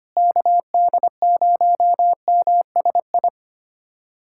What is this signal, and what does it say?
Morse code, heard as a keyed tone, about 25 wpm, 700 Hz. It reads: KD0MSI
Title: USA Callsign DE Callsign - 6 Characters - 25wpm